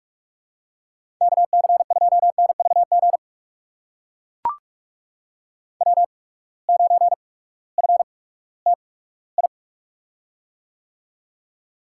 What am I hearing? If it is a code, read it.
Morse code, heard as a keyed tone, about 44 wpm, spaced as for 20 wpm, 700 Hz. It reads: KC2NVG W9FTI
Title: USA Call Signs - 8x Character Spacing - 45wpm